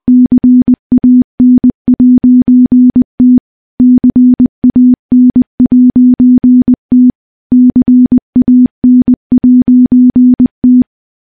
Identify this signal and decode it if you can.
Morse code, heard as a keyed tone, about 20 wpm, 250 Hz. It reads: CAN'T CAN'T CAN'T
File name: morse-training-20WPM